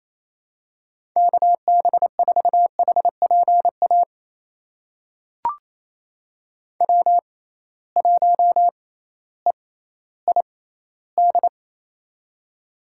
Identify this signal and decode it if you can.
Morse code, heard as a keyed tone, about 28 wpm, spaced as for 15 wpm, 700 Hz. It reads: KB4HPA W1EID
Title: USA Call Signs - 6x Character Spacing - 28wpm